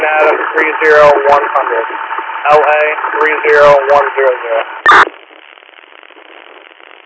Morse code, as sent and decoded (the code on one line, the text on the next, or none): none